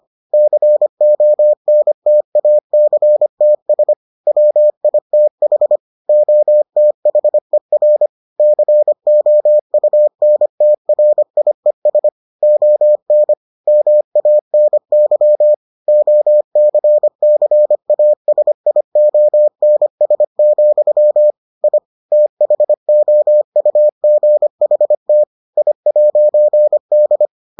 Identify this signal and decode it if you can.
Morse code, heard as a keyed tone, about 25 wpm, 600 Hz. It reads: CONTACTS WITH OTHER COUNTRIES ON MANY OCCASIONS, I THOUGHT I'D